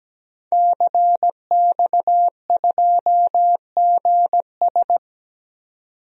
Morse code, heard as a keyed tone, about 17 wpm, 700 Hz. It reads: CX2GS
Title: DXCC-100 Call Signs - 5 Characters - 17wpm